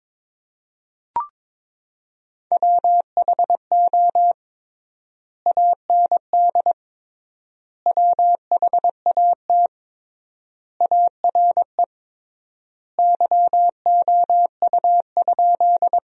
Morse code, heard as a keyed tone, about 22 wpm, 700 Hz. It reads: WHO AND WHAT ARE YOU?
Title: 3x Word-Spacing - Sentences from Top 100 Words - 22wpm